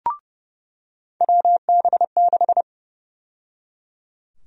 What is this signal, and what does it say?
Morse code, heard as a keyed tone, about 29 wpm, 700 Hz. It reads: WB6
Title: USA Call Sign Prefixes - 30wpm